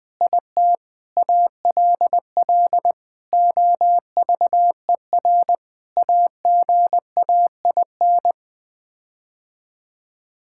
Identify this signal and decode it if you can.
Morse code, heard as a keyed tone, about 20 wpm, 700 Hz. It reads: IT ALL OVER AGAIN
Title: Sets of 9 Words - 20wpm